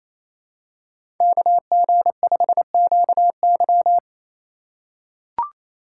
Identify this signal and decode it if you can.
Morse code, heard as a keyed tone, about 29 wpm, 700 Hz. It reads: KG5QY
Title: USA Call Signs - 5 Characters - 2x Character Spacing - 28wpm